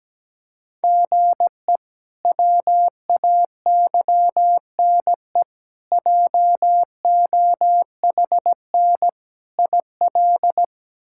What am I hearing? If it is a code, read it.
Morse code, heard as a keyed tone, about 17 wpm, 700 Hz. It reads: GE WAYNE JOHN IL
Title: K1USN SST QSO Practice - 17wpm